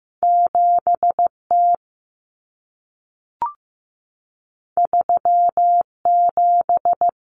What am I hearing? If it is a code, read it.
Morse code, heard as a keyed tone, about 15 wpm, 700 Hz. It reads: 7T 37
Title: Common Cut Numbers - 2-Digit 15wpm